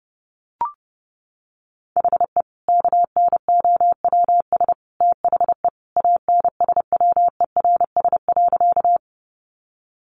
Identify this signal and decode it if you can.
Morse code, heard as a keyed tone, about 30 wpm, 700 Hz. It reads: HE KNOWS THE ANSWERS.
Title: Additional Easy Phrases 30wpm